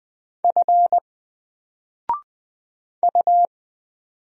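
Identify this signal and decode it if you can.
Morse code, heard as a keyed tone, about 20 wpm, 700 Hz. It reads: F U
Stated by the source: Single letter and number Morse drill at 20 wpm, from N4LCD.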